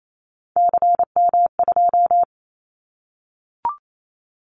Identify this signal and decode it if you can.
Morse code, heard as a keyed tone, about 28 wpm, 700 Hz. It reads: CM2
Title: Callsign Prefixes for Top 100 DXCC - 28wpm